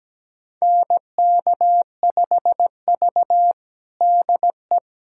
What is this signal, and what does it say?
Morse code, heard as a keyed tone, about 17 wpm, 700 Hz. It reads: NK5V DE
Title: USA Callsign DE Callsign - 4 Characters - 17wpm